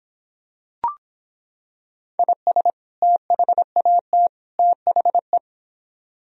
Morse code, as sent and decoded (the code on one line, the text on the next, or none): .. ... / - .... .- - / - .... .
IS THAT THE